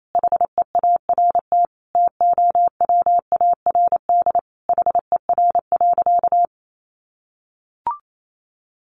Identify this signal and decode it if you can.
Morse code, heard as a keyed tone, about 28 wpm, 700 Hz. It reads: HEART TOWARD HER.
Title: Sentences from Top 1500 Words 28wpm